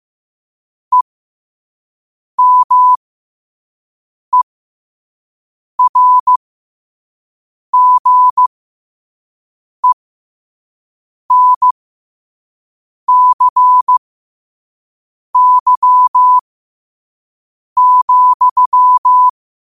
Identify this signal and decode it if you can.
Morse code, heard as a keyed tone, about 15 wpm, spaced as for 5 wpm, 1000 Hz. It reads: EMERGENCY,
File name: qotdfiles / 5wpm